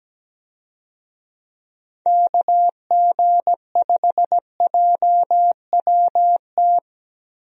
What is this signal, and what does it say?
Morse code, heard as a keyed tone, about 17 wpm, 700 Hz. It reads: KG5JWT
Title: USA Call Signs - 6 Characters - Repeated 3x - 17wpm